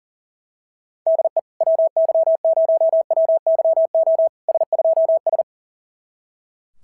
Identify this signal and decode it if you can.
Morse code, heard as a keyed tone, about 42 wpm, 650 Hz. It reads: DE WY0WYO S2S
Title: SOTA QSO Practice 40wpm